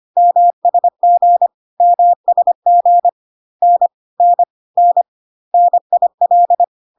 Morse code, heard as a keyed tone, about 25 wpm, 700 Hz. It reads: MSG MSG N N N NIL
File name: CW abbreviations alphabetical repeated 3 times 25 25wpm ik1hge